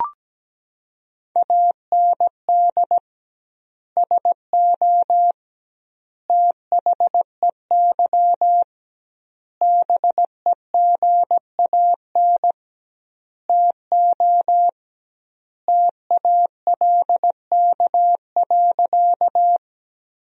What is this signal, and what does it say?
Morse code, heard as a keyed tone, about 17 wpm, 700 Hz. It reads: AND SO THEY BEGAN TO TALK.
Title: 2x Word-Spacing - Sentences from Top 400 Words - 17wpm